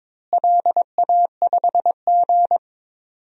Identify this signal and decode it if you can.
Morse code, heard as a keyed tone, about 22 wpm, 700 Hz. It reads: LA5G
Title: DXCC-100 Call Signs - 4 Characters - Repeated 3x - 22wpm